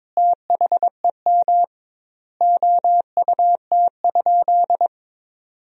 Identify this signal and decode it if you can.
Morse code, heard as a keyed tone, about 21 wpm, 700 Hz. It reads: THEM OUT?
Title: 2x Word-Spacing - Sentences from Top 300 Words - 22wpm